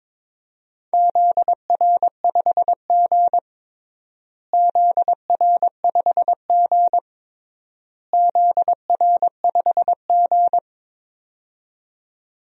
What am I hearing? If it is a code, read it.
Morse code, heard as a keyed tone, about 22 wpm, 700 Hz. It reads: ZR5G ZR5G ZR5G
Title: DXCC-100 Call Signs - 4 Characters - Repeated 3x - 22wpm